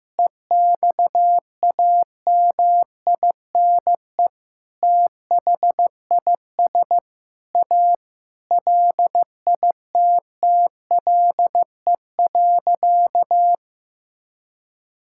Morse code, heard as a keyed tone, about 15 wpm, 700 Hz. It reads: EXAMINE THIS A LITTLE.
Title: Sentences from Top 1600 Words 15wpm